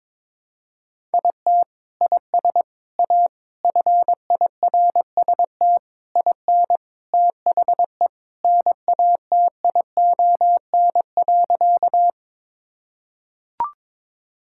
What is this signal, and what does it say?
Morse code, heard as a keyed tone, about 22 wpm, 700 Hz. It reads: IT IS A FIRST IN THE NATION.
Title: Sentences from Top 1000 Words - Review Part 6 - 22wpm